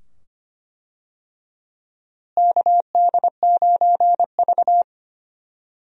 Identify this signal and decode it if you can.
Morse code, heard as a keyed tone, about 25 wpm, 700 Hz. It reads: KD9V
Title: USA Call Signs - 4 Characters - Repeated 2x - 25wpm